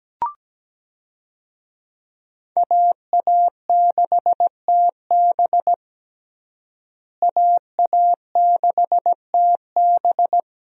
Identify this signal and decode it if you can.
Morse code, heard as a keyed tone, about 17 wpm, 700 Hz. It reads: AA6TB AA6TB
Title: USA Call Signs - 5 Characters - Repeated 2x - 17wpm